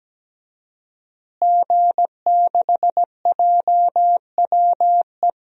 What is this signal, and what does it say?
Morse code, heard as a keyed tone, about 17 wpm, 700 Hz. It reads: G6JWE
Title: DXCC-100 Call Signs - 5 Characters - Repeated 3x - 17wpm